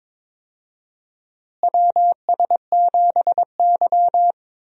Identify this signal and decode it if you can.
Morse code, heard as a keyed tone, about 22 wpm, 700 Hz. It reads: WS7Y
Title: USA Call Signs - 4 Characters - Repeated 2x - 22wpm